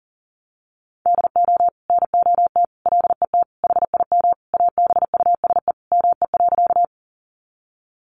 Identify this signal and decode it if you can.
Morse code, heard as a keyed tone, about 40 wpm, 700 Hz. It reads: DO NOT LET HIM ABUSE ME.
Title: Sentences from Top 1300 Words 40wpm